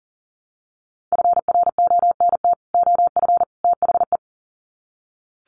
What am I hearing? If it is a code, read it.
Morse code, heard as a keyed tone, about 40 wpm, 700 Hz. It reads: FRONT OF THE
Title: Sets of 3 Words 40wpm